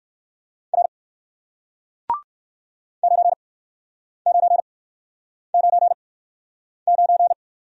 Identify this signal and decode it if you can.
Morse code, heard as a keyed tone, about 60 wpm, spaced as for 9 wpm, 700 Hz. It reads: R 9999